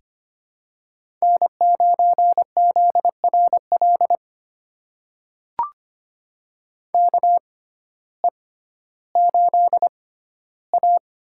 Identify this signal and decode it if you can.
Morse code, heard as a keyed tone, about 25 wpm, spaced as for 14 wpm, 700 Hz. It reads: N9ZRL KE8A
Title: USA Call Signs - 6x Character Spacing - 25wpm